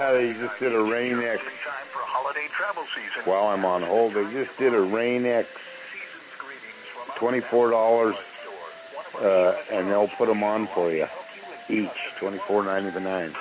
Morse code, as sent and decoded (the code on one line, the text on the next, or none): none